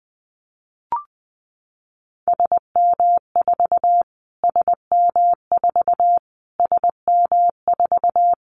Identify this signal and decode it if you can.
Morse code, heard as a keyed tone, about 20 wpm, 700 Hz. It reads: SM4 SM4 SM4
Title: Callsign Prefixes for Top 25 DXCC - Repeated 3x - 20wpm